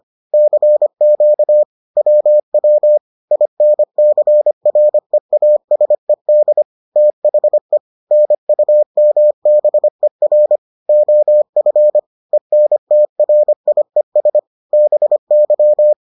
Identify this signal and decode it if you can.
Morse code, heard as a keyed tone, about 25 wpm, 600 Hz. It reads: CQ WW INCREASED THE NUMBER OF ENTRIES BY